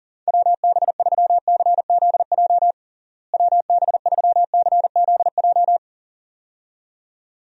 Morse code, heard as a keyed tone, about 40 wpm, 700 Hz. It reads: WB3CZJ WB3CZJ
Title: USA Call Signs - 6 Characters - Repeated 2x - 40wpm